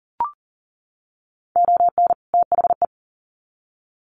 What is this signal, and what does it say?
Morse code, heard as a keyed tone, about 40 wpm, 700 Hz. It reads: ON THE
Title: Sets of 2 Words 40wpm